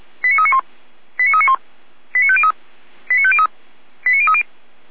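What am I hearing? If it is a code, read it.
Five-tone selective call (ZVEI), heard as quick runs of five tones, tones of about 70 ms. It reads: 89391 89391 89503 89503 89030
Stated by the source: ZVEI five-tone selective call received by SDR.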